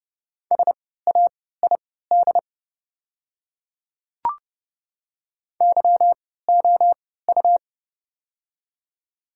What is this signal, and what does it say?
Morse code, heard as a keyed tone, about 31 wpm, spaced as for 17 wpm, 690 Hz. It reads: SAID YOU
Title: 3x Character Spacing - Top 300 Words - 30wpm